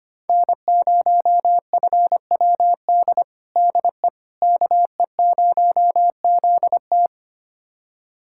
Morse code, heard as a keyed tone, about 25 wpm, 700 Hz. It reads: N0FWD DE KE0ZT
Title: USA Callsign DE Callsign - 5 Characters - 25wpm